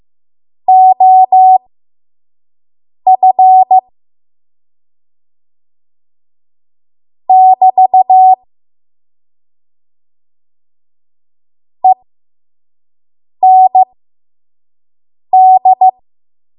Morse code, heard as a keyed tone, about 15 wpm, spaced as for 5 wpm, 750 Hz. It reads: OF <BT> END
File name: Morse / 05WPM